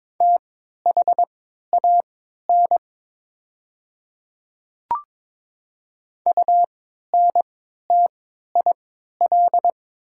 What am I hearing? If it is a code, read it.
Morse code, heard as a keyed tone, about 23 wpm, spaced as for 13 wpm, 700 Hz. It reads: THAN UNTIL
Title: 3x Character Spacing - Top 300 Words - 22wpm